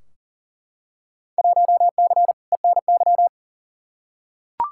Morse code, heard as a keyed tone, about 40 wpm, 700 Hz. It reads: 1C ENY
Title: Field Day Exchanges Encore - 40wpm